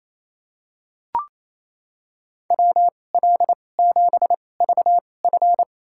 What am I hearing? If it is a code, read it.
Morse code, heard as a keyed tone, about 28 wpm, spaced as for 20 wpm, 700 Hz. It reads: WL7VF
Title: USA Call Signs - 5 Characters - 2x Character Spacing - 28wpm